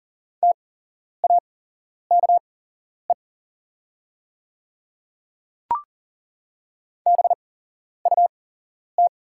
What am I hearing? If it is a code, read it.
Morse code, heard as a keyed tone, about 40 wpm, spaced as for 11 wpm, 690 Hz. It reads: TAKE BUT
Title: 8x Character Spacing - Top 200 Words - 40wpm